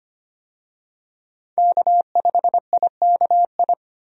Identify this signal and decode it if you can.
Morse code, heard as a keyed tone, about 25 wpm, 700 Hz. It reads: K5IKI
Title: USA Call Signs - 5 Characters - Repeated 3x - 25wpm